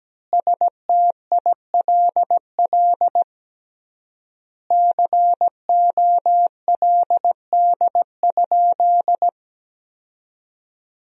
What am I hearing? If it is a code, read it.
Morse code, heard as a keyed tone, about 17 wpm, 700 Hz. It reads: STILL COLD?
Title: 3x Word-Spacing - Sentences from Top 400 Words - 17wpm